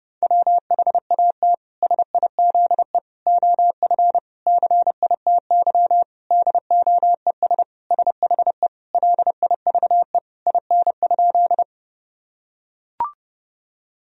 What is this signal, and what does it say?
Morse code, heard as a keyed tone, about 30 wpm, 700 Hz. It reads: WHAT SIZE OF CITY DOES SHE LIVE IN?
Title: Sentences from Top 1000 Words - Review Part 2 - 30wpm